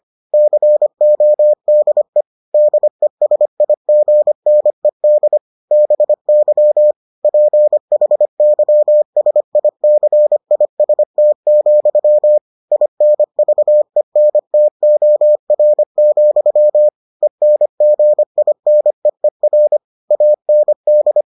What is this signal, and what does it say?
Morse code, heard as a keyed tone, about 25 wpm, 600 Hz. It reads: CODE DESIGNED BY PHYSICIST, INVENTOR, ENGINEER AND